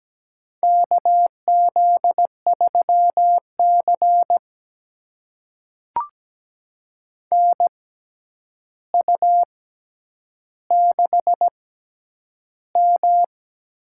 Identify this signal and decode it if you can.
Morse code, heard as a keyed tone, about 17 wpm, spaced as for 9 wpm, 700 Hz. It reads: KZ3C NU6M
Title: USA Call Signs - 4 Characters - 6x Character Spacing - 17wpm